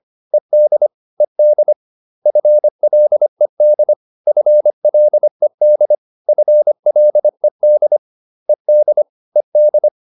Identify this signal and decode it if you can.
Morse code, heard as a keyed tone, about 25 wpm, 600 Hz. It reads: ED ED FLED FLED FLED ED ED